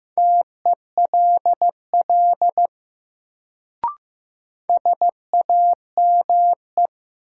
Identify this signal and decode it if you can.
Morse code, heard as a keyed tone, about 16 wpm, 700 Hz. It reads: TELL SAME
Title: Sets of 2 Words - New Word Drill - 15wpm